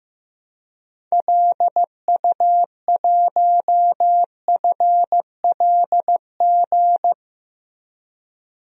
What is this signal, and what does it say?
Morse code, heard as a keyed tone, about 15 wpm, 700 Hz. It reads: LU1FLG
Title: DXCC-100 Call Signs - 6 Characters - Repeated 3x - 15wpm